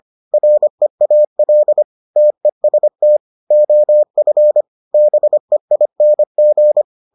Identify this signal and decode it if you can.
Morse code, heard as a keyed tone, about 25 wpm, 600 Hz. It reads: REAL TEST OF BEING